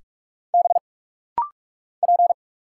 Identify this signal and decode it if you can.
Morse code, heard as a keyed tone, about 46 wpm, 700 Hz. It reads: B P
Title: Single Letter-Number - Rapid-Fire 45wpm